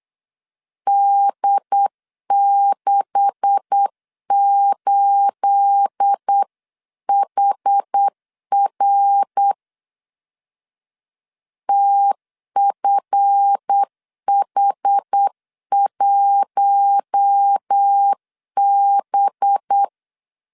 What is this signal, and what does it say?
Morse code, heard as a keyed tone, about 8 wpm, 800 Hz. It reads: D68HR TFH1B
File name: CW / DET / 1 9 1500C S C 8wpm 5653